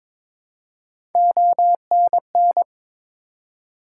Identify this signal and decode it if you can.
Morse code, heard as a keyed tone, about 22 wpm, 700 Hz. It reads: ONN
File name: ARRL Sections v2 - 22wpm